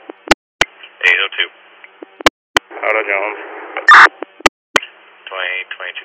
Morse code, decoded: none

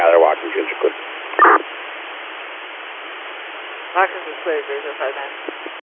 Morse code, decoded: none

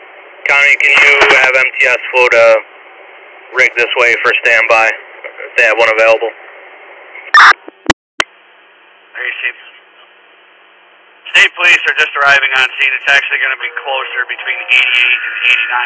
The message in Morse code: none

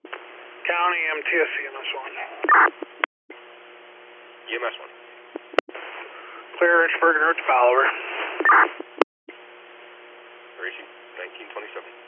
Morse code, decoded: none

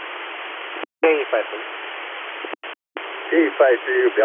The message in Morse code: none

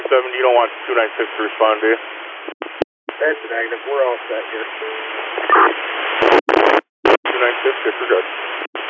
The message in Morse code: none

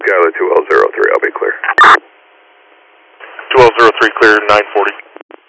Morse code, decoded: none

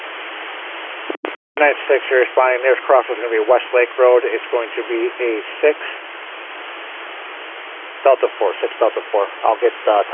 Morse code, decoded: none

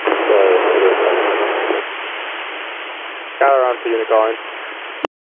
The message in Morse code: none

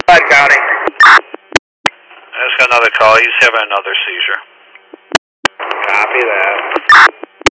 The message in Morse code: none

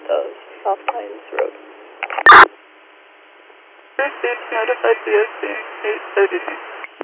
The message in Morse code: none

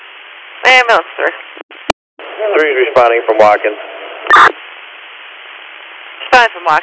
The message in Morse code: none

